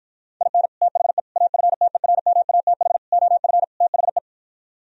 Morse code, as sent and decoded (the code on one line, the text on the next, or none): .. -. / - .... / .- ..-. - . .-. -- .- - .... / --- ..-. / - .... .
IN THE AFTERMATH OF THE